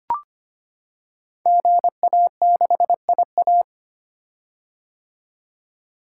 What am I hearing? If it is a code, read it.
Morse code, heard as a keyed tone, about 25 wpm, 700 Hz. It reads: GA6IA
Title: Code Groups - 5 Letter:Numbers - 25wpm